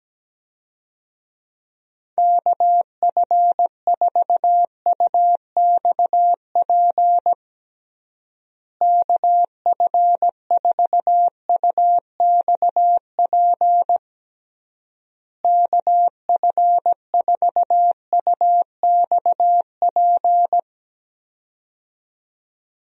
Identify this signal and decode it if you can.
Morse code, heard as a keyed tone, about 17 wpm, 700 Hz. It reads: KF4UXP KF4UXP KF4UXP
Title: USA Call Signs - 6 Characters - Repeated 3x - 17wpm